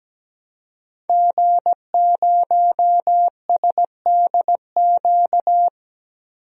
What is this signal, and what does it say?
Morse code, heard as a keyed tone, about 17 wpm, 700 Hz. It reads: G0SDQ
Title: DXCC-100 Call Signs - 5 Characters - Repeated 3x - 17wpm